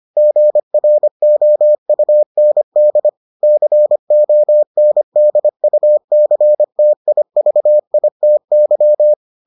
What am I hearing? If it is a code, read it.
Morse code, heard as a keyed tone, about 25 wpm, 600 Hz. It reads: GROUND CONDUCTIVITY